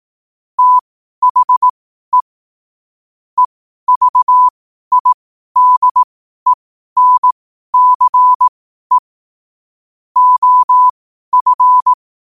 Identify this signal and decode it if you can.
Morse code, heard as a keyed tone, about 18 wpm, spaced as for 12 wpm, 1000 Hz. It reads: THE EVIDENCE OF
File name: qotdfiles / 12wpm